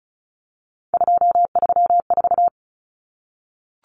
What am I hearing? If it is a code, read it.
Morse code, heard as a keyed tone, about 34 wpm, 700 Hz. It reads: 234
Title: Numbers - 3-Digit 35wpm